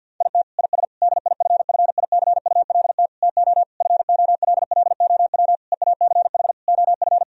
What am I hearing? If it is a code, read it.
Morse code, heard as a keyed tone, about 50 wpm, 700 Hz. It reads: IT IS DIFFICULT TO FOLLOW EACH OF
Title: Sentences from Top 1000 Words - Review Part 1 - 50wpm